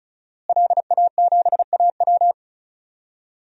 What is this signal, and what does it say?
Morse code, heard as a keyed tone, about 36 wpm, 700 Hz. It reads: LA7AW
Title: DXCC-100 Call Signs - 5 Characters - Repeated 3x - 35wpm